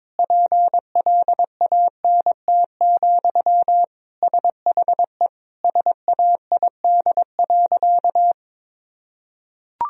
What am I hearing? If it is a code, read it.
Morse code, heard as a keyed tone, about 22 wpm, 700 Hz. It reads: PLANT, SHE SAID.
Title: Sentences from Top 1000 Words - Review Part 1 - 22wpm